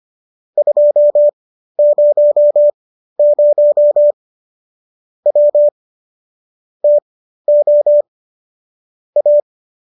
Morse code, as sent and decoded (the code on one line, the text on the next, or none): ..--- ----- ----- / .-- / - --- / .-
200 W TO A